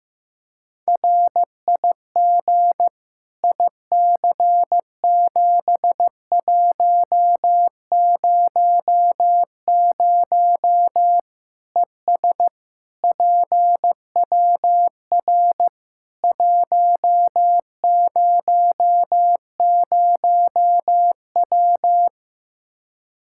Morse code, heard as a keyed tone, about 15 wpm, 700 Hz. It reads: RIG IC7100 ES PWR 100W
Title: Practice QSO - Exchange 2 - Info - 15wpm